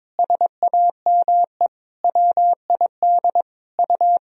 Morse code, heard as a keyed tone, about 21 wpm, 700 Hz. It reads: SAME WID U
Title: Sentences from Top 300 Words and Abbreviations 22wpm